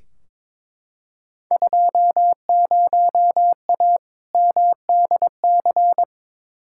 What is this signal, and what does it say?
Morse code, heard as a keyed tone, about 22 wpm, 700 Hz. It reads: 20A MDC
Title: Field Day Exchanges Encore - 22wpm